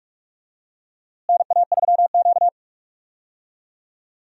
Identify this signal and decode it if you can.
Morse code, heard as a keyed tone, about 46 wpm, 700 Hz. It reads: NA3Q
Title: USA Call Signs - 4 Characters - Repeated 3x - 45wpm